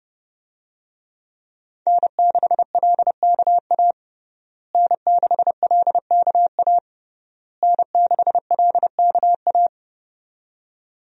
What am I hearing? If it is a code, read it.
Morse code, heard as a keyed tone, about 31 wpm, 700 Hz. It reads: N6LKA N6LKA N6LKA
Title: USA Call Signs - 5 Characters - Repeated 3x - 30wpm